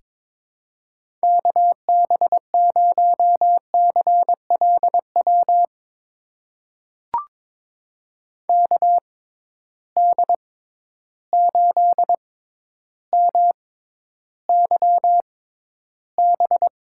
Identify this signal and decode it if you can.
Morse code, heard as a keyed tone, about 22 wpm, spaced as for 11 wpm, 700 Hz. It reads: KB0CLW KD8MYB
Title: USA Call Signs - 6 Characters - 6x Character Spacing - 22wpm